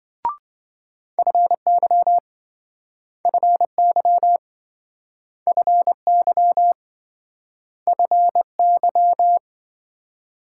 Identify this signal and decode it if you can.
Morse code, heard as a keyed tone, about 28 wpm, 700 Hz. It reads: FY FY FY FY